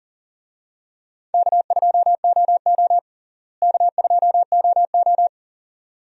Morse code, heard as a keyed tone, about 37 wpm, 700 Hz. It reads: K2OO K2OO
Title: USA Call Signs - 4 Characters - Repeated 2x - 40wpm